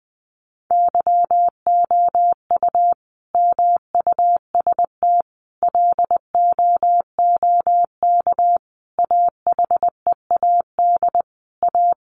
Morse code, heard as a keyed tone, about 20 wpm, 700 Hz. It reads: YOU MUST LOOK AHEAD A
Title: Sentences from Top 1100 Words 20wpm